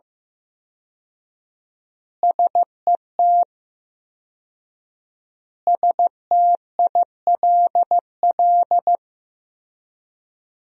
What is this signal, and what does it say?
Morse code, heard as a keyed tone, about 15 wpm, 700 Hz. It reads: SET STILL
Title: 4x Word-Spacing - Sentences from Top 300 Words - 15wpm